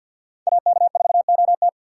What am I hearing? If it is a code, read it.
Morse code, heard as a keyed tone, about 67 wpm, spaced as for 58 wpm, 700 Hz. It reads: AK4OT
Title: Speed Racing - USA 2x2 Call Signs 50wpm